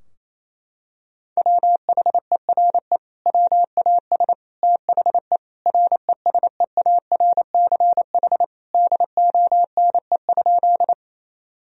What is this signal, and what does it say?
Morse code, heard as a keyed tone, about 28 wpm, 700 Hz. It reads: WHERE WAS THE RESEARCH DONE?
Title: Sentences from Top 1000 Words - Review Part 4 - 28wpm